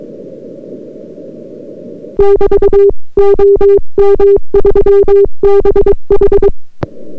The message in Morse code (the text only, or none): <BT> OM3BH